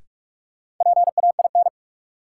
Morse code, heard as a keyed tone, about 46 wpm, 700 Hz. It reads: PAIN
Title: Sentences from Top 1600 Words - New Word Drill - 45wpm